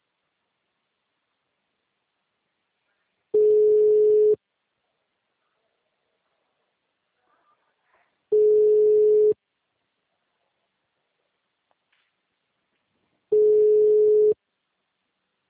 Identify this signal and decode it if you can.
Ringback tone (Europe, 425 Hz).
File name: out- -2011- - - .11407